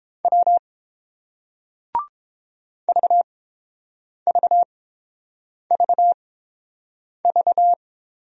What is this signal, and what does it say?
Morse code, heard as a keyed tone, about 30 wpm, spaced as for 7 wpm, 700 Hz. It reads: W VVVV